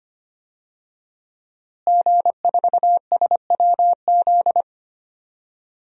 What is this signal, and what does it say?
Morse code, heard as a keyed tone, about 25 wpm, 700 Hz. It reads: G4SWZ
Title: DXCC-100 Call Signs - 5 Characters - Repeated 3x - 25wpm